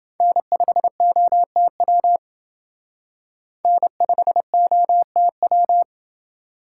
Morse code, heard as a keyed tone, about 28 wpm, 700 Hz. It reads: N5OTW N5OTW